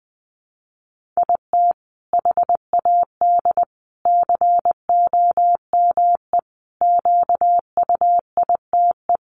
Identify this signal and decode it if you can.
Morse code, heard as a keyed tone, about 20 wpm, 700 Hz. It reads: IT HAD COME QUITE
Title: Sentences from Top 1300 Words 20wpm